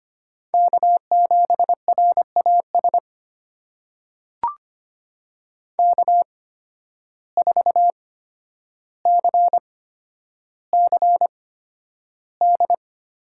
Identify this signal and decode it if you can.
Morse code, heard as a keyed tone, about 25 wpm, spaced as for 11 wpm, 700 Hz. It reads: K7RAS K4CCD
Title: USA Call Signs - 5 Characters - 8x Character Spacing - 25wpm